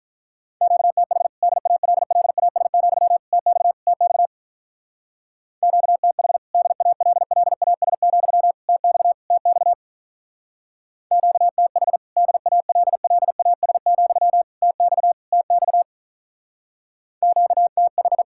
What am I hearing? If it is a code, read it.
Morse code, heard as a keyed tone, about 44 wpm, 700 Hz. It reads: QTH DALLAS, TX TX QTH DALLAS, TX TX QTH DALLAS, TX TX QTH